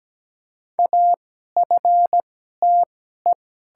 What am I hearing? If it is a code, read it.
Morse code, heard as a keyed tone, about 17 wpm, spaced as for 12 wpm, 700 Hz. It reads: AFTE
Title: 2x Character Spacing - Top 200 Words - 17wpm